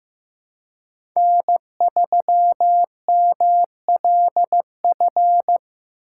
Morse code, heard as a keyed tone, about 15 wpm, 700 Hz. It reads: N3MLF
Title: USA Call Signs - 5 Characters - 2x Character Spacing - 15wpm